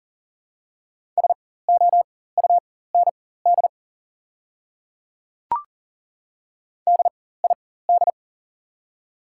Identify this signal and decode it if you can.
Morse code, heard as a keyed tone, about 42 wpm, spaced as for 19 wpm, 700 Hz. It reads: SOUND DID
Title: 4x Character Spacing - Top 100 Words - 40wpm